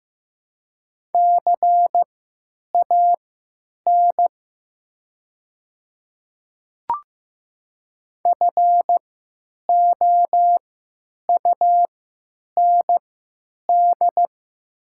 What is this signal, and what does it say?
Morse code, heard as a keyed tone, about 15 wpm, spaced as for 9 wpm, 700 Hz. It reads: CAN FOUND
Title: 3x Character Spacing - Top 300 Words - 15wpm